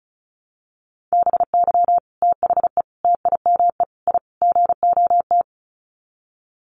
Morse code, heard as a keyed tone, about 34 wpm, 700 Hz. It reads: BY THE TIME I GOT
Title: Sets of 5 Words Encore2 35wpm